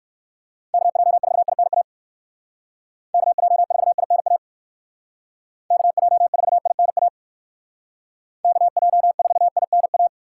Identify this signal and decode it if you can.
Morse code, heard as a keyed tone, about 58 wpm, 700 Hz. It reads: KJ4INA KJ4INA KJ4INA KJ4INA